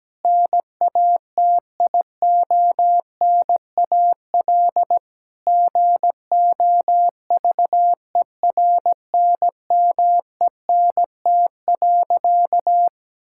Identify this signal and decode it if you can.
Morse code, heard as a keyed tone, about 17 wpm, 700 Hz. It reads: NATIONAL GOVERNMENT.
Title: Sentences from Top 1000 Words - Review Part 6 - 17wpm